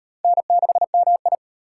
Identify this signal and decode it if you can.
Morse code, heard as a keyed tone, about 36 wpm, 700 Hz. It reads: N6MI